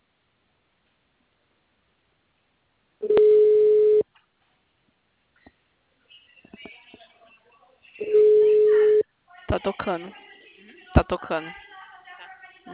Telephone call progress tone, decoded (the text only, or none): ringback tone